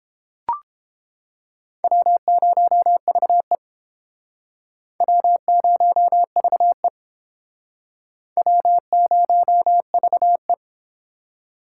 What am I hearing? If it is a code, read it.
Morse code, heard as a keyed tone, about 30 wpm, 700 Hz. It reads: W0VE W0VE W0VE